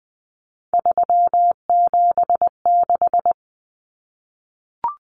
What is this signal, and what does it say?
Morse code, heard as a keyed tone, about 20 wpm, 700 Hz. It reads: 376
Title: Common Cut Numbers - 3-Digit 20wpm